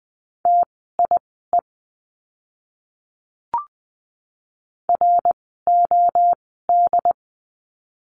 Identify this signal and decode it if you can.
Morse code, heard as a keyed tone, about 20 wpm, spaced as for 15 wpm, 700 Hz. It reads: TIE ROD
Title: 068 Lesson 12f - Letter d - Words 20wpm